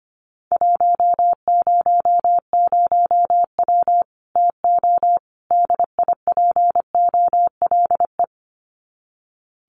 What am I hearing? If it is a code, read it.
Morse code, heard as a keyed tone, about 25 wpm, 700 Hz. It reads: 100W TO DIPOLE
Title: Common QSO Phrases 25wpm